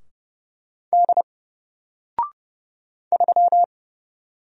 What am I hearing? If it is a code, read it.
Morse code, heard as a keyed tone, about 30 wpm, 700 Hz. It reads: D 3
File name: Single-Letter-Number-30wpm